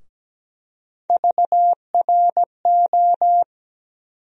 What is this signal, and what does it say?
Morse code, heard as a keyed tone, about 17 wpm, 700 Hz. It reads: VRO
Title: USA Call Sign Postfixes - 3 Letters 17wpm